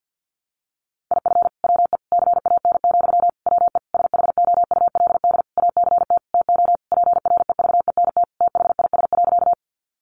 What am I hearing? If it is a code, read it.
Morse code, heard as a keyed tone, about 50 wpm, 700 Hz. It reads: IF WE CAN, WE SHOULD ACT TO PREVENT THIS.
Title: Sentences from Top 1100 Words 50wpm